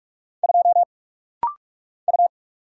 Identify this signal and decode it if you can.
Morse code, heard as a keyed tone, about 44 wpm, 700 Hz. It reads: 2 U
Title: Single Letter-Number - Rapid-Fire 45wpm